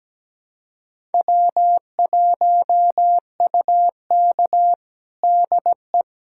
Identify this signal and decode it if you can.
Morse code, heard as a keyed tone, about 17 wpm, 700 Hz. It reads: W1UK DE
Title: USA Callsign DE Callsign - 4 Characters - 17wpm